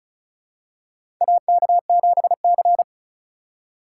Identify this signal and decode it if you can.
Morse code, heard as a keyed tone, about 36 wpm, 700 Hz. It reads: AK7C
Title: USA Call Signs - 4 Characters - 3x Character Spacing - 35wpm